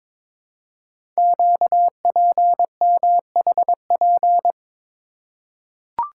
Morse code, heard as a keyed tone, about 23 wpm, 700 Hz. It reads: QPMHP
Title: Code Groups - 5 Letters - 22wpm